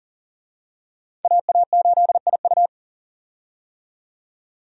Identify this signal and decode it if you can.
Morse code, heard as a keyed tone, about 40 wpm, 700 Hz. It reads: AA8IU
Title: USA Call Signs - 5 Characters - 40wpm